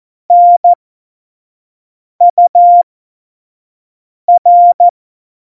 Morse code, heard as a keyed tone, about 14 wpm, spaced as for 5 wpm, 700 Hz. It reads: NUR
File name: Morsekurs / Witze / Witz 25. Dezember 2018, 25 F70